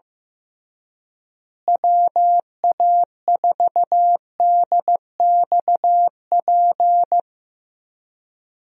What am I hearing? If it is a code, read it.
Morse code, heard as a keyed tone, about 15 wpm, 700 Hz. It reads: WA4DXP
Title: USA Call Signs - 6 Characters - Repeated 3x - 15wpm